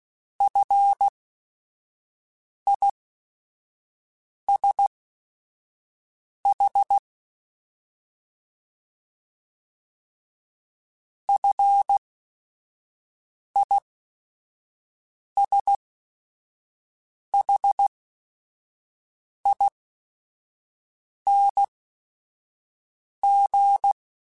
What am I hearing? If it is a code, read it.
Morse code, heard as a keyed tone, about 16 wpm, spaced as for 5 wpm, 800 Hz. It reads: FISH FISHING
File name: Short Runs / Short Runs-mp3s / SinglePrefixes @5